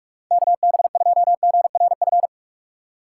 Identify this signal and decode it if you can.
Morse code, heard as a keyed tone, about 41 wpm, 700 Hz. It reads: KB2GRF
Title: USA Call Signs - 6 Characters - 4x Character Spacing - 45wpm